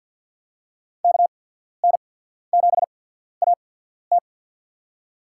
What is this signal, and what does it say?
Morse code, heard as a keyed tone, about 53 wpm, spaced as for 14 wpm, 700 Hz. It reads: KN7AT
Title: USA Call Signs - 5 Characters - 8x Character Spacing - 50wpm